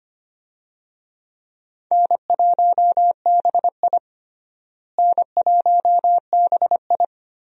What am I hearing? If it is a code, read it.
Morse code, heard as a keyed tone, about 25 wpm, 700 Hz. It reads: N1BI N1BI
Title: USA Call Signs - 4 Characters - Repeated 2x - 25wpm